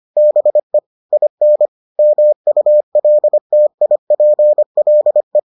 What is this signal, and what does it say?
Morse code, heard as a keyed tone, about 25 wpm, 600 Hz. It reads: BE IN MULTIPLE